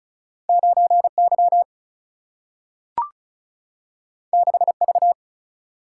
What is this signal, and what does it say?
Morse code, heard as a keyed tone, about 36 wpm, 700 Hz. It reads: 9Y 6V
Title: Single-Letter Single-Number 35wpm